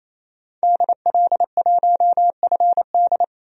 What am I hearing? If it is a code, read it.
Morse code, heard as a keyed tone, about 27 wpm, 700 Hz. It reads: DL1FD
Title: DXCC-100 Call Signs - 5 Characters - 28wpm